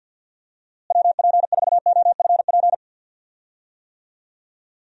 Morse code, heard as a keyed tone, about 52 wpm, 700 Hz. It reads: WP4OFP
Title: USA Call Signs - 6 Characters - Repeated 2x - 50wpm